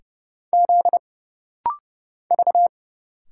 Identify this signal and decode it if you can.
Morse code, heard as a keyed tone, about 30 wpm, 700 Hz. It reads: Z V
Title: Single Letters - Rapid-Fire 30wpm